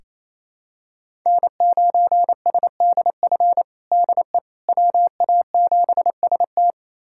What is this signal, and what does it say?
Morse code, heard as a keyed tone, about 28 wpm, 700 Hz. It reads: N9SDF DE WA7ST
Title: USA Callsign DE Callsign - 5 Characters - 28wpm